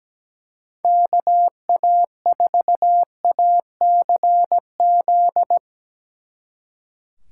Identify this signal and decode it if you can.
Morse code, heard as a keyed tone, about 18 wpm, 700 Hz. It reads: KA4ACZ
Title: Speed Racing - USA 2x3 Call Signs 17wpm